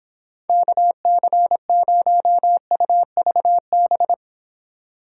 Morse code, heard as a keyed tone, about 31 wpm, 700 Hz. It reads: KC0UVB